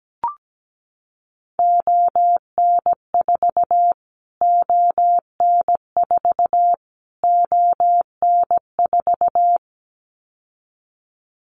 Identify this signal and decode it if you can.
Morse code, heard as a keyed tone, about 17 wpm, 700 Hz. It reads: ON4 ON4 ON4
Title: Callsign Prefixes for Top 25 DXCC - Repeated 3x - 17wpm